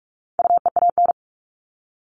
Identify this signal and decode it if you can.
Morse code, heard as a keyed tone, about 44 wpm, 700 Hz. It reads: UEAN
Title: All Cut Numbers - 4-Digit 45wpm